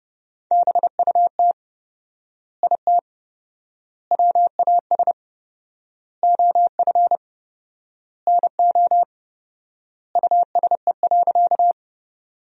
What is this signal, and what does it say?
Morse code, heard as a keyed tone, about 30 wpm, 700 Hz. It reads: BUT IT WAS OF NO USE.
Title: 4x Word-Spacing - Sentences from Top 100 Words - 30wpm